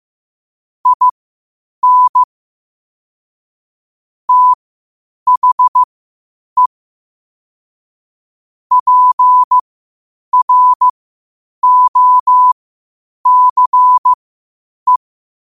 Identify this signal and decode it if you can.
Morse code, heard as a keyed tone, about 15 wpm, spaced as for 8 wpm, 1000 Hz. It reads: IN THE PROCE
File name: qotdfiles / 8wpm